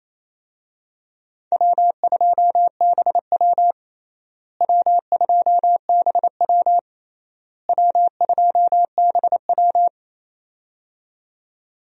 Morse code, heard as a keyed tone, about 28 wpm, 700 Hz. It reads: W2BW W2BW W2BW
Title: USA Call Signs - 4 Characters - Repeated 3x - 28wpm